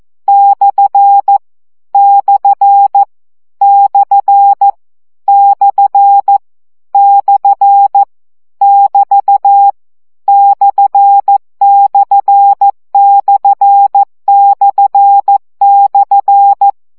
Morse code, heard as a keyed tone, about 14 wpm, 800 Hz. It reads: / / / / / <BT> /////
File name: CW gyakorlas per 12 baud